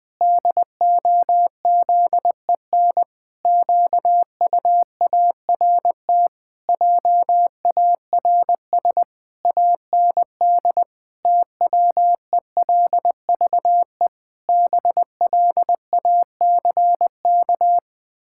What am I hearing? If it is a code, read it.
Morse code, heard as a keyed tone, about 21 wpm, 700 Hz. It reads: DOZEN QUART JARS AND TWELVE BLACK
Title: Pangram Challenge v2 - 20wpm